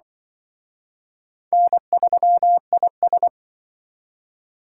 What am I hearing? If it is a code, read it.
Morse code, heard as a keyed tone, about 25 wpm, 690 Hz. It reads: N3IS
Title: Speed Racing - USA 1x2 Call Signs 20wpm